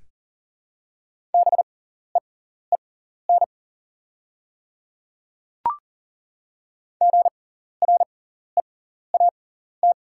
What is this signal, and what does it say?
Morse code, heard as a keyed tone, about 40 wpm, spaced as for 14 wpm, 690 Hz. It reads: BEEN GREAT
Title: 6x Character Spacing - Top 300 Words - 40wpm